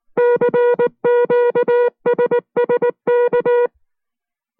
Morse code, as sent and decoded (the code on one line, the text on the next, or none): -.-. --.- ... ... -.-
CQSSK